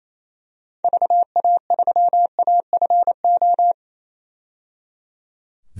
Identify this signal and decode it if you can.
Morse code, heard as a keyed tone, about 28 wpm, 700 Hz. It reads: VA3AFO
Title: DXCC-100 Call Signs - 6 Characters - 28wpm